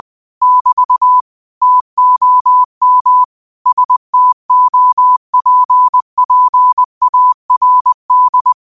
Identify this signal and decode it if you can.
Morse code, heard as a keyed tone, about 20 wpm, 1000 Hz. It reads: <BT> TOM STOPPARD
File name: qotdfiles / 20wpm